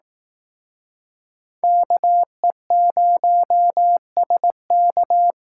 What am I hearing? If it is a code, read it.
Morse code, heard as a keyed tone, about 18 wpm, 700 Hz. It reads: KE0SK
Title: Speed Racing - USA 2x2 Call Signs 15wpm